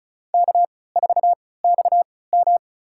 Morse code, heard as a keyed tone, about 36 wpm, spaced as for 20 wpm, 700 Hz. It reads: K4XM
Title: USA Call Signs - 4 Characters - 3x Character Spacing - 35wpm